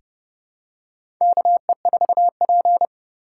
Morse code, heard as a keyed tone, about 29 wpm, 700 Hz. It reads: KE4P
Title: USA Call Signs - 4 Characters - 2x Character Spacing - 30wpm